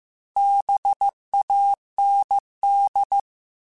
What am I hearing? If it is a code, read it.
Morse code, heard as a keyed tone, about 15 wpm, 800 Hz. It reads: BAND
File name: QSOs / QSOs-mp3s / WD0BGP 1@10